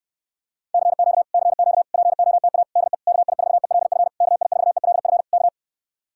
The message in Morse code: CQ CQ POTA DE KI4ELF KI4ELF K

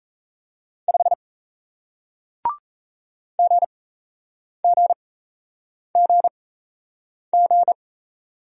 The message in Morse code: 5 GGGG

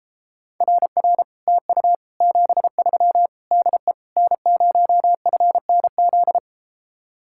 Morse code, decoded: RR TU 73 DE N0FNZ

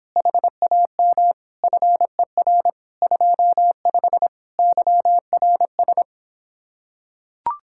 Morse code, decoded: HAM FER 25 YRS

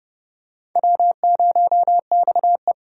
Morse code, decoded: W0XE

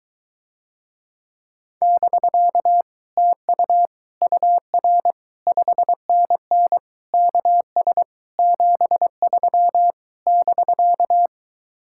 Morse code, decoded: <BK> TU UR 5NN KS 73 <BK>

